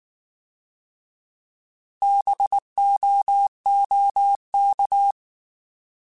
Morse code, heard as a keyed tone, about 19 wpm, 800 Hz. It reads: BOOK
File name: Alphabetical / Alphabetical mp3s / Bs @18